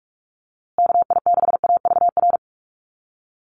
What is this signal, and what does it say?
Morse code, heard as a keyed tone, about 44 wpm, 700 Hz. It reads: KI6AVR
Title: USA Call Signs Repeated 2x 45wpm